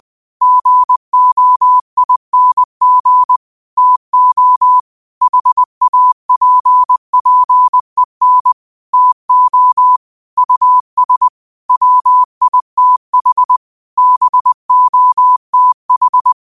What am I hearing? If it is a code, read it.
Morse code, heard as a keyed tone, about 20 wpm, 1000 Hz. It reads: GOING TO HAPPEN TO US WITH BOTH
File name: qotdfiles / 20wpm